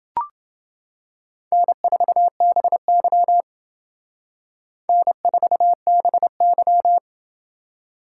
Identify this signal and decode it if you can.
Morse code, heard as a keyed tone, about 28 wpm, 700 Hz. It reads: N4BY N4BY